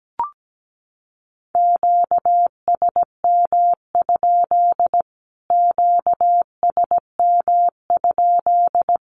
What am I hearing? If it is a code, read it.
Morse code, heard as a keyed tone, about 17 wpm, 700 Hz. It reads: QSM? QSM?
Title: Q-Codes Repeated 3x 17wpm